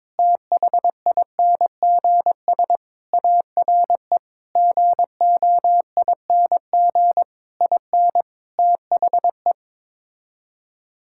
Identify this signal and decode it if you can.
Morse code, heard as a keyed tone, about 22 wpm, 700 Hz. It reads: THINGS ARE GOING IN THE
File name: practice / 22wpm / Sets of 8 Words - 22wpm